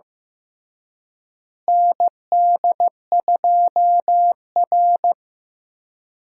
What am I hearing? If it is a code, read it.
Morse code, heard as a keyed tone, about 16 wpm, 700 Hz. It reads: ND2R
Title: USA Call Signs - 4 Characters - Repeated 3x - 15wpm